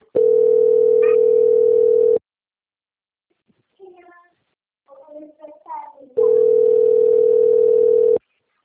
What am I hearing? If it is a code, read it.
Ringback tone (North America), 440 + 480 Hz.
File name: out- - - - - .5705